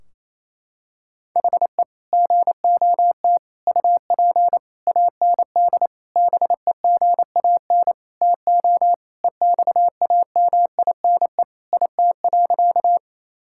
Morse code, heard as a keyed tone, about 28 wpm, 700 Hz. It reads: HE GOT UP AND BEGAN TO EXAMINE IT.
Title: Sentences from Top 1600 Words 28wpm